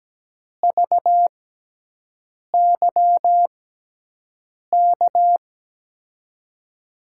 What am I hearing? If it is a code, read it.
Morse code, heard as a keyed tone, about 17 wpm, spaced as for 6 wpm, 700 Hz. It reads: VYK
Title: USA Call Signs - 6 Characters - 6x Character Spacing - 17wpm